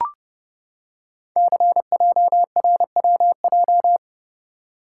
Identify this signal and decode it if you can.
Morse code, heard as a keyed tone, about 30 wpm, 690 Hz. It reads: CJRWJ
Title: Code Groups - 5 Letters - 30wpm